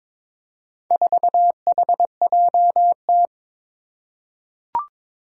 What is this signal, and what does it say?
Morse code, heard as a keyed tone, about 22 wpm, 700 Hz. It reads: 4HJT
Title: Four Challenging Letter-Number Combinations 22wpm